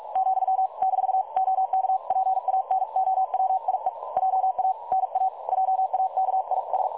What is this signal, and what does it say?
Morse code, heard as a keyed tone, about 46 wpm, 760 Hz. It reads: , -OK PANOWIE JA NA JAKIS
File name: dlazaawansowanych / nagraniazpasma / websdr recording 3565.0kHz